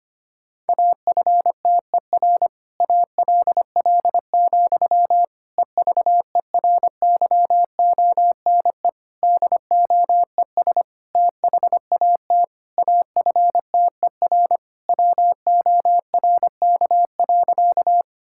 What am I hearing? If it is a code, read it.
Morse code, heard as a keyed tone, about 25 wpm, 700 Hz. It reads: AFTER ALL, EVERYONE DOES THAT AFTER WORK.
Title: Sentences from Top 1000 Words - Review Part 5 - 25wpm